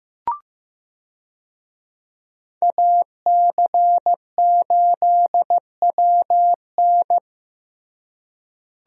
Morse code, heard as a keyed tone, about 15 wpm, 700 Hz. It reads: AC8WN
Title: USA Call Signs - 5 Characters - Repeated 2x - 15wpm